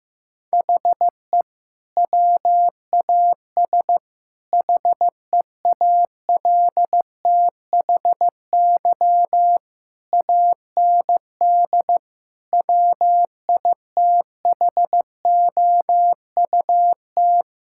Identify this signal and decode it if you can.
Morse code, heard as a keyed tone, about 15 wpm, 700 Hz. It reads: HE WAS HEALTHY AND WITHOUT
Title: Sentences from Top 1000 Words - Review Part 6 - 15wpm